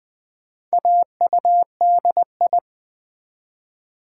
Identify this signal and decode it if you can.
Morse code, heard as a keyed tone, about 20 wpm, 700 Hz. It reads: AUDI
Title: Build A Word - 4 Letters At A Time - 8-Letter Words - 20wpm